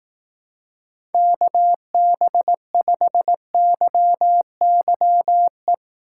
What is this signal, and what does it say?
Morse code, heard as a keyed tone, about 18 wpm, 700 Hz. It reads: KB5YYE
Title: Speed Racing - USA Call Signs 15wpm